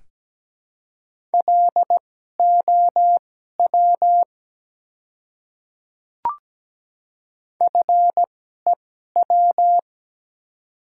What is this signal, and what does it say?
Morse code, heard as a keyed tone, about 17 wpm, spaced as for 12 wpm, 700 Hz. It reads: LOW FEW
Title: 2x Character Spacing - Top 300 Words - 17wpm